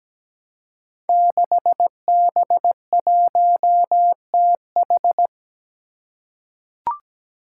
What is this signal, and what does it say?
Morse code, heard as a keyed tone, about 17 wpm, 700 Hz. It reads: 6B1TH
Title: Five Challenging Letter-Number Combinations 17wpm